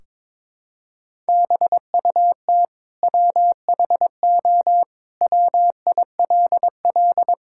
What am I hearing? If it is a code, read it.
Morse code, heard as a keyed tone, about 22 wpm, 700 Hz. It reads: BUT WHO WILL
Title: Sentences from Top 1600 Words 22wpm